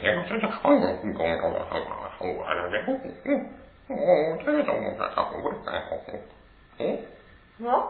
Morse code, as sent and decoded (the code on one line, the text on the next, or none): none